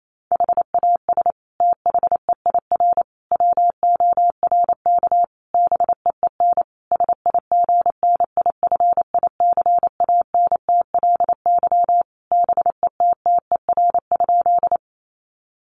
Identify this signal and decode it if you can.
Morse code, heard as a keyed tone, about 28 wpm, 700 Hz. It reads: HAS THEIR WORK BEEN SIGNIFICANTLY BETTER?
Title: Sentences from Top 1500 Words 28wpm